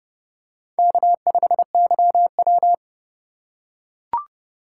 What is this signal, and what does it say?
Morse code, heard as a keyed tone, about 29 wpm, 700 Hz. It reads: K5YW